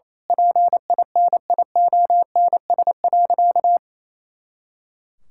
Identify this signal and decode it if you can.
Morse code, heard as a keyed tone, about 28 wpm, 700 Hz. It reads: PINIONS.
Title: Pangram Challenge v2 - 28wpm